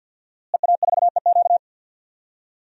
Morse code, heard as a keyed tone, about 48 wpm, 700 Hz. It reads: EA4EQ
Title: DXCC-100 Call Signs - 5 Characters - Repeated 3x - 50wpm